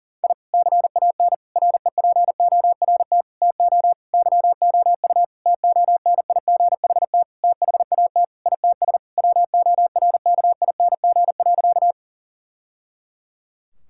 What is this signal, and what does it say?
Morse code, heard as a keyed tone, about 37 wpm, 700 Hz. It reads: I CAN REPORT TO YOU TONIGHT THAT ITS WORKING.
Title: Sentences from Top 1000 Words - Review Part 6 - 40wpm